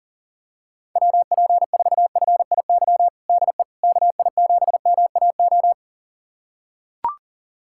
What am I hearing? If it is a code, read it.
Morse code, heard as a keyed tone, about 40 wpm, 700 Hz. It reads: WP4FIY DE KI7MAO
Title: USA Callsign DE Callsign 40wpm